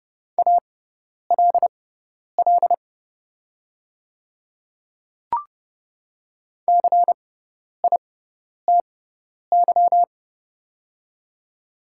Morse code, heard as a keyed tone, about 30 wpm, spaced as for 10 wpm, 700 Hz. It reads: ALL CITY
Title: 6x Character Spacing - Top 500 Words - 30wpm